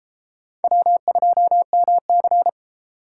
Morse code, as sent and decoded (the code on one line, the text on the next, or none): .-- ..--- -- -.-.
W2MC